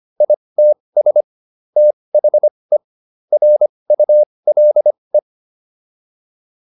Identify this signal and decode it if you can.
Morse code, heard as a keyed tone, about 25 wpm, spaced as for 20 wpm, 600 Hz. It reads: ITS THE RULE
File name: PR303 20